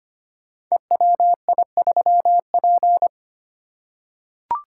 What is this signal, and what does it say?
Morse code, heard as a keyed tone, about 25 wpm, 700 Hz. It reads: EWI3P
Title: Code Groups - 5 Letter:Numbers - 25wpm